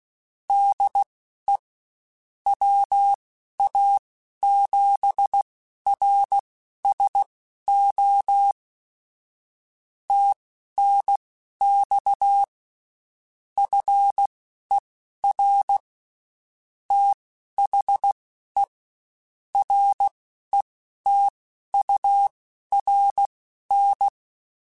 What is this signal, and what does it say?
Morse code, heard as a keyed tone, about 16 wpm, spaced as for 11 wpm, 800 Hz. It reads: DE WA7RSO TNX FER THE RETURN